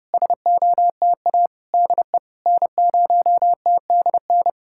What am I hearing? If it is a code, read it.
Morse code, heard as a keyed tone, about 30 wpm, 700 Hz. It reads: SOTA DE N0TDN
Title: SOTA QSO Practice 30wpm